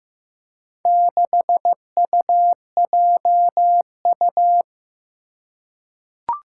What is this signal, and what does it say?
Morse code, heard as a keyed tone, about 16 wpm, 700 Hz. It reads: 6UJU
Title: Four Challenging Letter-Number Combinations 15wpm